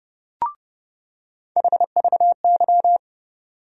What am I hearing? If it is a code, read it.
Morse code, heard as a keyed tone, about 30 wpm, 700 Hz. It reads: HVY
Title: USA Call Sign Postfixes - 3 Letters 30wpm